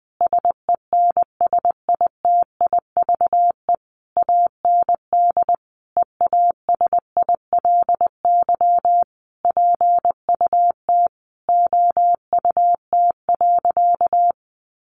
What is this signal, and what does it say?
Morse code, heard as a keyed tone, about 20 wpm, 700 Hz. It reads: SENSITIVE AND EASILY PUT OUT.
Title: Sentences from Top 1400 Words 20wpm